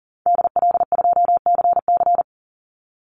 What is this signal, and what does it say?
Morse code, heard as a keyed tone, about 41 wpm, 700 Hz. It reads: DL2CC
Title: Call Signs - 40wpm